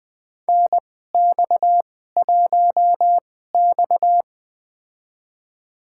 Morse code, heard as a keyed tone, about 20 wpm, spaced as for 15 wpm, 700 Hz. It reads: NX1X
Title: USA Call Signs - 2x Character Spacing - 20wpm